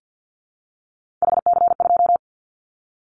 Morse code, heard as a keyed tone, about 50 wpm, 700 Hz. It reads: HC2
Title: Callsign Prefixes for Top 100 DXCC - 50wpm